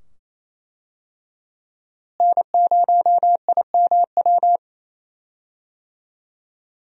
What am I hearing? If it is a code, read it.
Morse code, heard as a keyed tone, about 27 wpm, 700 Hz. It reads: N0IMW
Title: USA Call Signs - 5 Characters - Repeated 2x - 28wpm